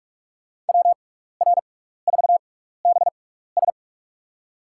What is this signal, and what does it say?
Morse code, heard as a keyed tone, about 46 wpm, spaced as for 16 wpm, 700 Hz. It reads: WR4BS
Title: USA Call Signs - 5 Characters - 6x Character Spacing - 45wpm